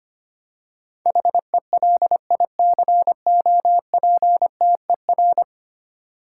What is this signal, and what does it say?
Morse code, heard as a keyed tone, about 25 wpm, 700 Hz. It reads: HELICOPTER
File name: 10 Letter Words - First 3 Letters Given - 25wpm